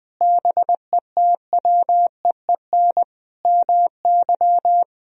Morse code, heard as a keyed tone, about 20 wpm, 700 Hz. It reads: BETWEEN MY
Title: Sentences from Top 1000 Words - Review Part 1 - 20wpm